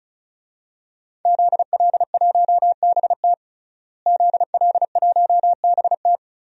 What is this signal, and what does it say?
Morse code, heard as a keyed tone, about 36 wpm, 700 Hz. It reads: ZL1BT ZL1BT
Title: DXCC-100 Call Signs - 5 Characters - Repeated 2x - 35wpm